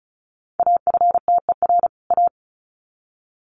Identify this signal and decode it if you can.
Morse code, heard as a keyed tone, about 34 wpm, 700 Hz. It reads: AFTER A
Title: Sets of 2 Words 35wpm